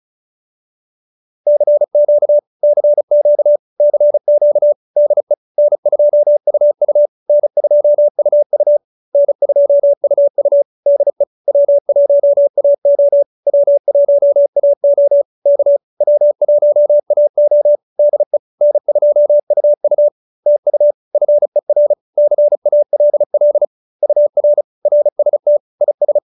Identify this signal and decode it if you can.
Morse code, heard as a keyed tone, about 34 wpm, 590 Hz. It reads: CQ CQ CQ DE N2UU N2UU N2UU DE W1AO W1AO K W1AO DE N2UU TU FER CALL UR RST IS
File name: QSO313 35